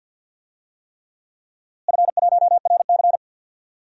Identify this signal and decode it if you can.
Morse code, heard as a keyed tone, about 51 wpm, 700 Hz. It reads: F1RC
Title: DXCC-100 Call Signs - 4 Characters - Repeated 3x - 50wpm